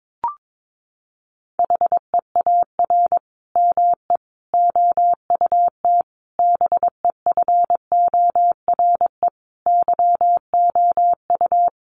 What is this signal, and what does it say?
Morse code, heard as a keyed tone, about 22 wpm, 700 Hz. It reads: HEAR ME OUT BEFORE YOU
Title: Sentences from Top 1400 Words 22wpm